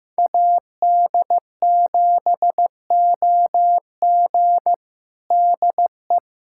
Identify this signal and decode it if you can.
Morse code, heard as a keyed tone, about 16 wpm, 700 Hz. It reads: AD7OG DE
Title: USA Callsign DE Callsign - 5 Characters - 15wpm